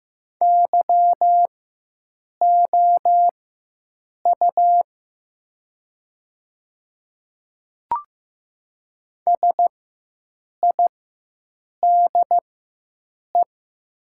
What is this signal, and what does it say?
Morse code, heard as a keyed tone, about 15 wpm, spaced as for 7 wpm, 700 Hz. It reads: YOU SIDE
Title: 4x Character Spacing - Top 100 Words - 15wpm